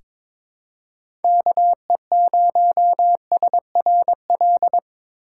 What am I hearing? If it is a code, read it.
Morse code, heard as a keyed tone, about 22 wpm, 700 Hz. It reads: KE0SRL
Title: K1USN SST QSO Practice - 22wpm